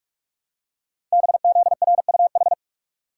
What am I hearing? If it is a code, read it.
Morse code, heard as a keyed tone, about 44 wpm, 700 Hz. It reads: BZRUH
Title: Code Groups - 5 Letters - 45wpm